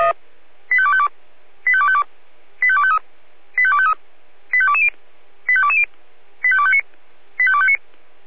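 Five-tone selective call (ZVEI), heard as quick runs of five tones, tones of about 70 ms. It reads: 85252 85252 85253 85253 85220 85220 85259 85259
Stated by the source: ZVEI five-tone call received on SDR.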